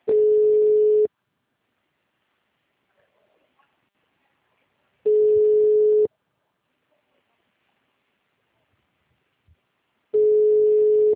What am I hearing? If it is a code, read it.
Ringback tone (Europe, 425 Hz).